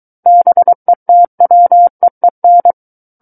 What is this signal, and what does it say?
Morse code, heard as a keyed tone, about 23 wpm, 700 Hz. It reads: BETWEEN
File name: Book One / 14-Chapter 14 - 23 WPM